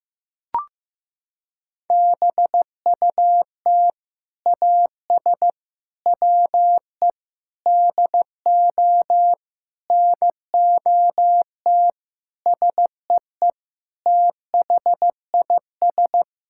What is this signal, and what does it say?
Morse code, heard as a keyed tone, about 16 wpm, 700 Hz. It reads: BUT AS WE DO NOT SEE THIS
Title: Sentences from Top 1600 Words 15wpm